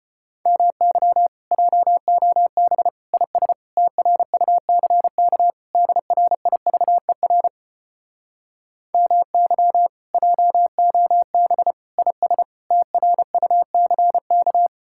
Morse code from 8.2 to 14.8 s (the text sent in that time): MY JOB IS TRUCK